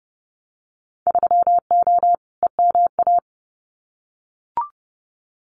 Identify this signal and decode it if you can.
Morse code, heard as a keyed tone, about 30 wpm, 700 Hz. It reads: 3O EMA
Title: Winter Field Day Exchanges 30wpm